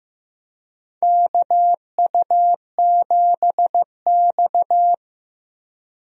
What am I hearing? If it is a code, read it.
Morse code, heard as a keyed tone, about 15 wpm, 700 Hz. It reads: KU7X
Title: USA Call Signs - 4 Characters - Repeated 2x - 15wpm